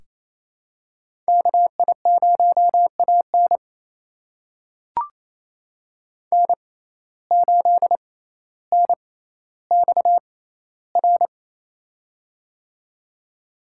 Morse code, heard as a keyed tone, about 28 wpm, spaced as for 14 wpm, 700 Hz. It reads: KI0AN N8NXR
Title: USA Call Signs - 5 Characters - 6x Character Spacing - 28wpm